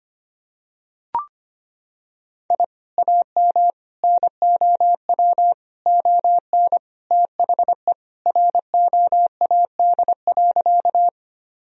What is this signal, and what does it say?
Morse code, heard as a keyed tone, about 25 wpm, 700 Hz. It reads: I AM NOW ON THE ROAD.
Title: Sentences from Top 1000 Words - Review Part 2 - 25wpm